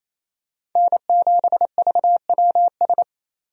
Morse code, heard as a keyed tone, about 28 wpm, 700 Hz. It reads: N7VWS
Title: USA Call Signs - 5 Characters - 28wpm